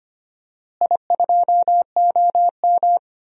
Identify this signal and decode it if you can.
Morse code, heard as a keyed tone, about 25 wpm, 700 Hz. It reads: I2OM
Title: DXCC-100 Call Signs - 4 Characters - Repeated 3x - 25wpm